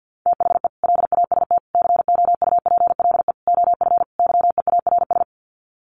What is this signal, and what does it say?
Morse code, heard as a keyed tone, about 50 wpm, 700 Hz. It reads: THE LAST COUPLE OF YEARS
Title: Sets of 5 Words 50wpm